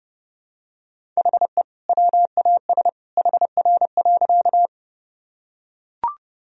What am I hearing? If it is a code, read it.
Morse code, heard as a keyed tone, about 30 wpm, 700 Hz. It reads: HE WAS HR.
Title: Sentences from Top 200 Words and Abbreviations 30wpm